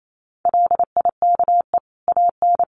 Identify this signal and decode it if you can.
Morse code, heard as a keyed tone, about 28 wpm, 700 Hz. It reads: LIKE AN
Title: Sentences from Top 1500 Words 28wpm